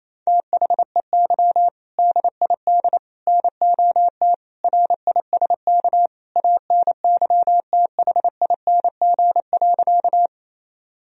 Morse code, heard as a keyed tone, about 28 wpm, 700 Hz. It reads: THEY DID NOT RISK ANYTHING.
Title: Sentences from Top 1000 Words - Review Part 4 - 28wpm